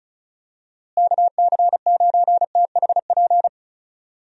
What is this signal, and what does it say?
Morse code, heard as a keyed tone, about 34 wpm, 700 Hz. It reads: KC9THP
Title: USA Call Signs - 6 Characters - Repeated 2x - 35wpm